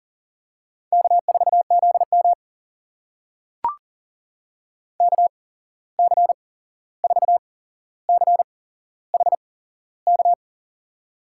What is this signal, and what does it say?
Morse code, heard as a keyed tone, about 40 wpm, spaced as for 15 wpm, 700 Hz. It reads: K4ZM KC4CHK
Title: USA Call Signs - 8x Character Spacing - 40wpm